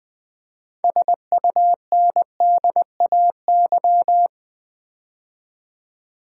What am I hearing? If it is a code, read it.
Morse code, heard as a keyed tone, about 20 wpm, 700 Hz. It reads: SUNDAY
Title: Build A Word - 3 Letters At A Time - 6-Letter Words - 20wpm